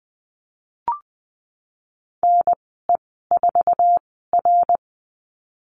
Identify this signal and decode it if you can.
Morse code, heard as a keyed tone, about 20 wpm, spaced as for 15 wpm, 700 Hz. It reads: NE4R
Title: 128 Lesson 21g - Number 6 - Call Signs 20wpm